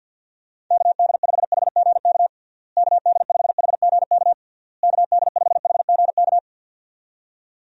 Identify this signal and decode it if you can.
Morse code, heard as a keyed tone, about 52 wpm, 700 Hz. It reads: KD5HGK KD5HGK KD5HGK
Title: USA Call Signs - 6 Characters - Repeated 3x - 50wpm